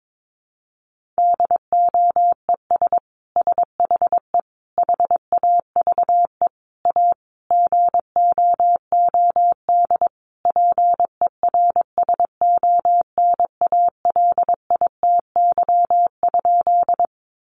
Morse code, heard as a keyed tone, about 22 wpm, 700 Hz. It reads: DOES SHE HAVE A GOOD PERSONALITY?
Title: Sentences from Top 1400 Words 22wpm